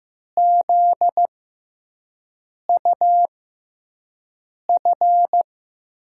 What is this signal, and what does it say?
Morse code, heard as a keyed tone, about 15 wpm, spaced as for 5 wpm, 700 Hz. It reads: ZUF
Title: USA Call Signs - 6 Characters - 6x Character Spacing - 15wpm